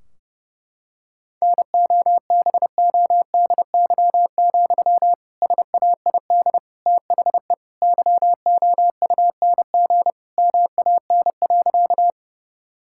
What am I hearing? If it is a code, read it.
Morse code, heard as a keyed tone, about 32 wpm, 700 Hz. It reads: NOBODY, SAID THE YOUNG MAN.
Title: Sentences from Top 1600 Words 30wpm